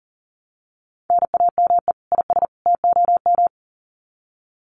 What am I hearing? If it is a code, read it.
Morse code, heard as a keyed tone, about 40 wpm, 700 Hz. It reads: NAME IS TOM
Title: Common QSO Phrases 40wpm